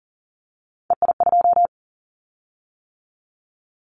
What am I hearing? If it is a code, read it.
Morse code, heard as a keyed tone, about 40 wpm, 700 Hz. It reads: EI2
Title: Callsign Prefixes for Top 50 DXCC - 40wpm